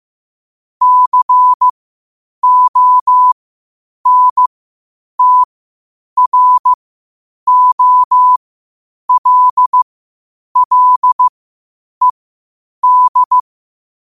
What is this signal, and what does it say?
Morse code, heard as a keyed tone, about 15 wpm, spaced as for 8 wpm, 1000 Hz. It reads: CONTROLLED
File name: qotdfiles / 8wpm